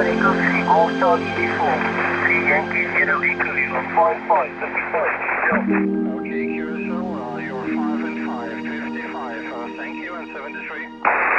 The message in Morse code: none